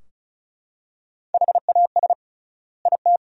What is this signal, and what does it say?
Morse code, heard as a keyed tone, about 34 wpm, 700 Hz. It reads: HAS IT
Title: 3x Word-Spacing - Sentences from Top 100 Words - 35wpm